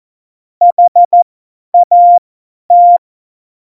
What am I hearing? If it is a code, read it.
Morse code, heard as a keyed tone, about 14 wpm, spaced as for 10 wpm, 700 Hz. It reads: HAT
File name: Morsekurs / Witze / Witz 15. Oktober 2019, 50 F70